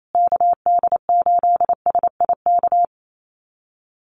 Morse code, heard as a keyed tone, about 28 wpm, 700 Hz. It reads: KD8SIK
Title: USA Call Signs Repeated 3x 28wpm